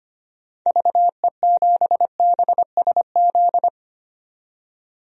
Morse code, heard as a keyed tone, about 25 wpm, 700 Hz. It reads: VE7BSZ
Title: DXCC-100 Call Signs - 6 Characters - 25wpm